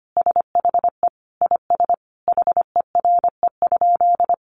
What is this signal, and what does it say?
Morse code, heard as a keyed tone, about 25 wpm, 700 Hz. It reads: SHE IS HERE?
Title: Sentences from Top 1300 Words 25wpm